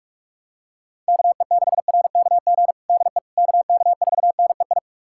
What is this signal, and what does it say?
Morse code, heard as a keyed tone, about 41 wpm, 700 Hz. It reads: KE6RKG DE KK4NEI
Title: USA Callsign DE Callsign 45wpm